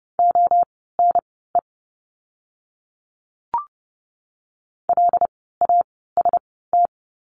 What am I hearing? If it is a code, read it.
Morse code, heard as a keyed tone, about 30 wpm, spaced as for 17 wpm, 700 Hz. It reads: ONE LAST
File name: Lesson 17f - Letter u - Words 30wpm